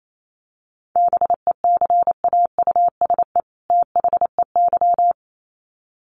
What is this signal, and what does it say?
Morse code, heard as a keyed tone, about 28 wpm, 700 Hz. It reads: BECAUSE THEY
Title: Sets of 2 Words 28wpm